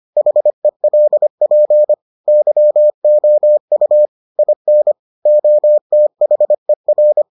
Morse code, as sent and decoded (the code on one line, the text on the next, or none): .... .-.. .--. / -.-- --- ..- / .. -. / --- - .... .-.
HELP YOU IN OTHER